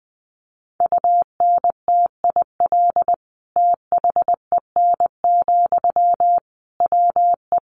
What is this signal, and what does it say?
Morse code, heard as a keyed tone, about 20 wpm, 700 Hz. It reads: UNTIL THEN, WE
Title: Sentences from Top 1500 Words 20wpm